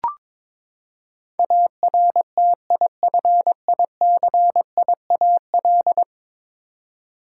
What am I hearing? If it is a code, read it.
Morse code, heard as a keyed tone, about 22 wpm, 700 Hz. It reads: ARTIFICIAL
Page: morse-code-ninja-practice-22wpm